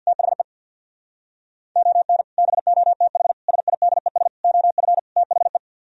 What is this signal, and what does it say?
Morse code, heard as a keyed tone, about 64 wpm, 700 Hz. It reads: THE ON BOTH SIDES OF THE